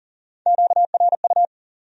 Morse code, heard as a keyed tone, about 54 wpm, spaced as for 47 wpm, 700 Hz. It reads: QRU
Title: Speed-Racing Q-Codes - 40wpm